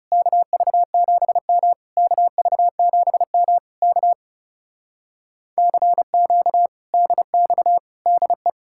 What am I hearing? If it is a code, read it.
Morse code, heard as a keyed tone, about 40 wpm, 700 Hz. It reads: KV7M KV7M K CQ DX DE